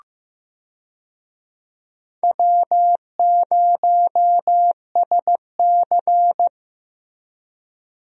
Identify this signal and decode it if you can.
Morse code, heard as a keyed tone, about 16 wpm, 700 Hz. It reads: W0SC
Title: USA Call Signs - 4 Characters - Repeated 3x - 15wpm